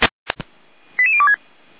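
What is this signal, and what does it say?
Five-tone selective call (ZVEI), heard as quick runs of five tones, tones of about 70 ms. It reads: 88026
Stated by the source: ZVEI five-tone selective call on 455, received by SDR.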